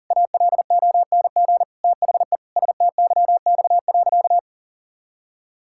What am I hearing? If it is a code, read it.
Morse code, heard as a keyed tone, about 40 wpm, 700 Hz. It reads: ALONG THE STYX.
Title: Pangram Challenge v2 - 40wpm